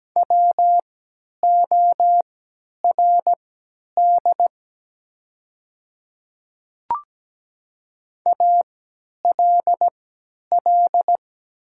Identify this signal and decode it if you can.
Morse code, heard as a keyed tone, about 17 wpm, spaced as for 10 wpm, 700 Hz. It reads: WORD ALL
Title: 3x Character Spacing - Top 100 Words - 17wpm